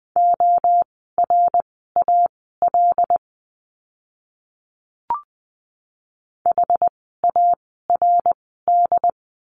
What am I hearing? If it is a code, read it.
Morse code, heard as a keyed tone, about 20 wpm, spaced as for 14 wpm, 700 Hz. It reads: ORAL HARD
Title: 075 Lesson 13f - Letter L - Words 20wpm